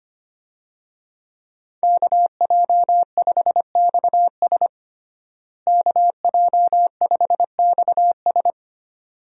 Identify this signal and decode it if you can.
Morse code, heard as a keyed tone, about 25 wpm, 700 Hz. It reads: KJ5XS KJ5XS
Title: USA Call Signs - 5 Characters - Repeated 2x - 25wpm